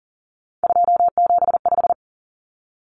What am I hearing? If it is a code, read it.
Morse code, heard as a keyed tone, about 40 wpm, 700 Hz. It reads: 275
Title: Common Cut Numbers - 3-Digit 40wpm